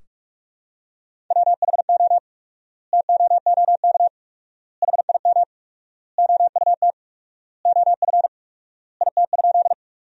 Morse code, heard as a keyed tone, about 45 wpm, 700 Hz. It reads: WHO TOOK HIM OUT OF IT?
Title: 4x Word-Spacing - Sentences from Top 300 Words - 45wpm